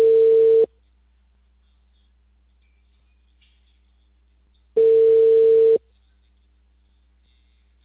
Ringback tone (China, 450 Hz).